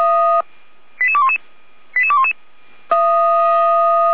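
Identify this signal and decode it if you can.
Five-tone selective call (ZVEI), heard as quick runs of five tones, tones of about 70 ms. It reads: 88211 88211
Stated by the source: ZVEI five-tone selective call 88235 received by SDR, from the feuerwehr-orlamuende (Orlamünde fire brigade) audio server.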